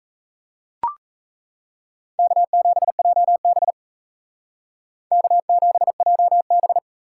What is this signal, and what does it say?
Morse code, heard as a keyed tone, about 35 wpm, 700 Hz. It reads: K7JB K7JB
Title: Speed Racing - USA 1x2 Call Signs 28wpm